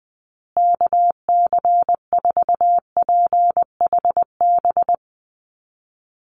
Morse code, heard as a keyed tone, about 19 wpm, 700 Hz. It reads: KC4PHB
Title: USA Call Signs Repeated 2x 20wpm